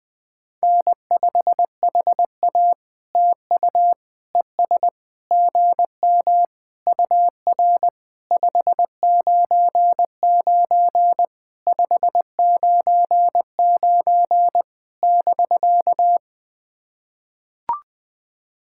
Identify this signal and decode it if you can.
Morse code, heard as a keyed tone, about 19 wpm, 700 Hz. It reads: N5HA TU ES GM UR 599 599 <BK>
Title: POTA QSO Practice - 20wpm